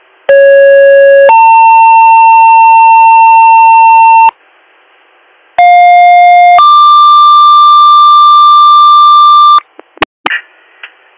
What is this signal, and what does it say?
Two-tone page, heard as a one-second tone, then a three-second tone, about 550 then 900 Hz, about 710 then 1180 Hz. It is heard from 0.3 to 9.6 s.